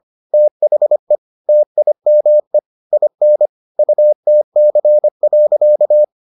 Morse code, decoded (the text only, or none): THE TIME IN UTC.